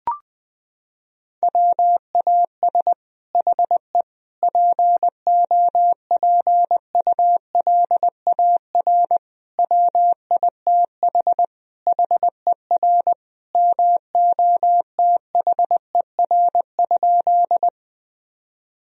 Morse code, heard as a keyed tone, about 21 wpm, 700 Hz. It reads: WAS HE POPULAR WITH HER MOTHER?
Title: Sentences from Top 1000 Words - Review Part 3 - 20wpm